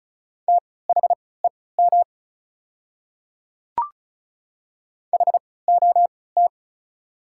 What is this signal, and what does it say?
Morse code, heard as a keyed tone, about 36 wpm, spaced as for 20 wpm, 690 Hz. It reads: THEM HOT
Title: 3x Character Spacing - Top 300 Words - 35wpm